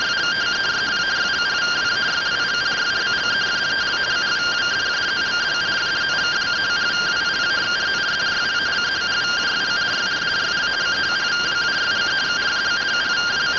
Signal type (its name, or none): two-tone teleprinter signal